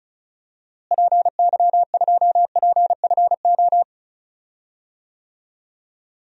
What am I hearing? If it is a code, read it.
Morse code, heard as a keyed tone, about 34 wpm, 700 Hz. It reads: PY2PFO
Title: DXCC-100 Call Signs - 6 Characters - Repeated 2x - 35wpm